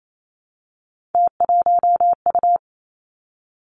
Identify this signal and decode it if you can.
Morse code, heard as a keyed tone, about 28 wpm, 700 Hz. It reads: T1U
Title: Three Challenging Letter-Number Combinations 28wpm